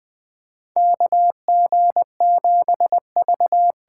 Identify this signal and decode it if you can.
Morse code, heard as a keyed tone, about 20 wpm, 700 Hz. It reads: KG7V
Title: USA Call Signs - 6 Characters - 8x Character Spacing - 20wpm